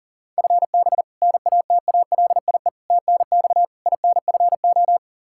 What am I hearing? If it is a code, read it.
Morse code, heard as a keyed tone, about 40 wpm, 700 Hz. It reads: FB NATALIE TNX INFO
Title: Practice QSO - Exchange 3 - More Info - 40wpm